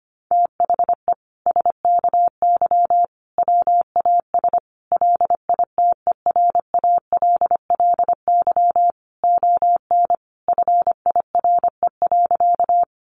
Morse code, heard as a keyed tone, about 25 wpm, 700 Hz. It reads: THE SKY WAS LITERALLY ON FIRE.
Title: Sentences from Top 1400 Words 25wpm